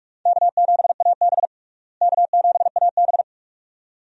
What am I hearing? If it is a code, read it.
Morse code, heard as a keyed tone, about 47 wpm, 700 Hz. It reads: K7AB K7AB
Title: USA Call Signs - 4 Characters - Repeated 2x - 45wpm